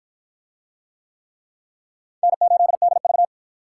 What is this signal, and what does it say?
Morse code, heard as a keyed tone, about 53 wpm, 700 Hz. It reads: N8DV